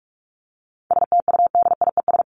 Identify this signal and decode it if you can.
Morse code, heard as a keyed tone, about 44 wpm, 700 Hz. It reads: STUDIES
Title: ES:S Suffix - Top 100 Words with Base Words 45wpm